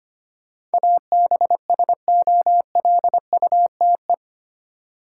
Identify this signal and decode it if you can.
Morse code, heard as a keyed tone, about 25 wpm, 700 Hz. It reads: ABSOLUTE
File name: 8 Letter Words - First 3 Letters Given - 25wpm